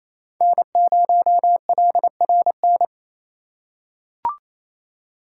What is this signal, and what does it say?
Morse code, heard as a keyed tone, about 28 wpm, 700 Hz. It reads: N0LRN
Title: USA Call Signs - 5 Characters - 2x Character Spacing - 28wpm